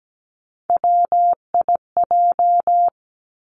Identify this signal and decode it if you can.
Morse code, heard as a keyed tone, about 17 wpm, 700 Hz. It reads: WIJ
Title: Three Challenging Letter-Number Combinations 17wpm